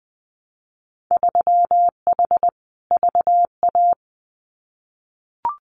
Morse code, heard as a keyed tone, about 20 wpm, 700 Hz. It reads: 3H VA
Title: Winter Field Day Exchanges Encore 20wpm